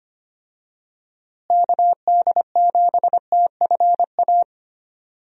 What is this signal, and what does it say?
Morse code, heard as a keyed tone, about 25 wpm, 700 Hz. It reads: KD7TFA
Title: USA Call Signs - 6 Characters - Repeated 3x - 25wpm